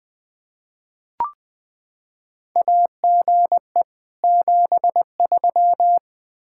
Morse code, heard as a keyed tone, about 20 wpm, 700 Hz. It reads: AGE 73
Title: Practice QSO - Exchange 3 - More Info - 20wpm